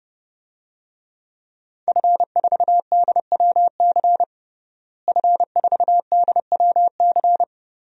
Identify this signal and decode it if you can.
Morse code, heard as a keyed tone, about 30 wpm, 700 Hz. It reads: F4DWC F4DWC
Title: DXCC-100 Call Signs - 5 Characters - Repeated 2x - 30wpm